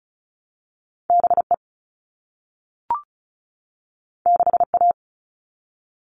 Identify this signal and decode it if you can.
Morse code, heard as a keyed tone, about 35 wpm, 700 Hz. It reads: BE 6A
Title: All Cut Numbers - 2-Digit 35wpm